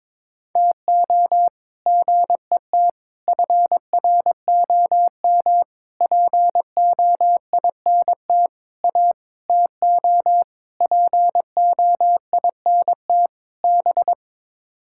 Morse code, heard as a keyed tone, about 21 wpm, 700 Hz. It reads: TO GET FROM POINT A TO POINT B
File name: practice / 22wpm / Sets of 8 Words - 22wpm